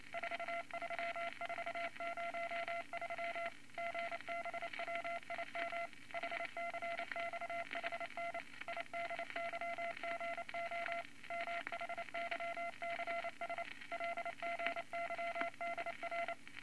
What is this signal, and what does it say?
Morse code, heard as a keyed tone, about 28 wpm, 680 Hz, with noise. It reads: 43403 GBWIM HCXHN IDYGO MHYCS LGYDR
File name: Morse Code Records / 22 A LCM 172 Twenty Words Per Minute